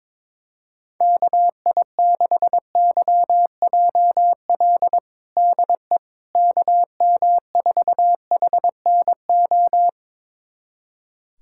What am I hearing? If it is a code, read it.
Morse code, heard as a keyed tone, about 22 wpm, 700 Hz. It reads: KI6YJL DE KM4HNO
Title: USA Callsign DE Callsign - 6 Characters - 22wpm